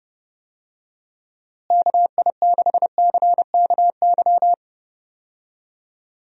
Morse code, heard as a keyed tone, about 30 wpm, 700 Hz. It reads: KI6CKY
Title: USA Call Signs - 6 Characters - Repeated 3x - 30wpm